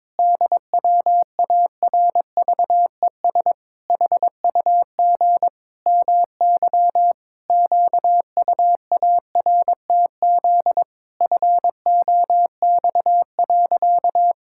Morse code, heard as a keyed tone, about 22 wpm, 700 Hz. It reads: DWARVES HUG MY QUARTZ FOX.
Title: Pangram Challenge v2 - 22wpm